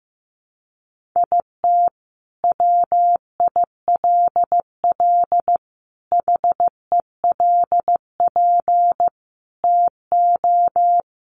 Morse code, heard as a keyed tone, about 15 wpm, 700 Hz. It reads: IT WILL HELP TO
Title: Sentences from Top 1400 Words 15wpm